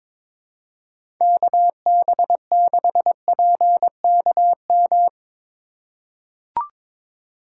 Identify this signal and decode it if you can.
Morse code, heard as a keyed tone, about 22 wpm, 700 Hz. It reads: KB6PKM
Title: USA Call Signs - 4x Character Spacing - 22wpm